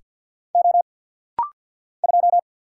Morse code, heard as a keyed tone, about 50 wpm, 690 Hz. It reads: O 2
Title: Single Letter-Number Alt - Rapid-Fire 50wpm